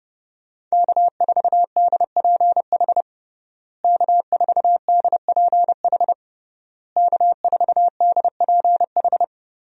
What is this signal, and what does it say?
Morse code, heard as a keyed tone, about 30 wpm, 700 Hz. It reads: K4DPH K4DPH K4DPH
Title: USA Call Signs - 5 Characters - Repeated 3x - 30wpm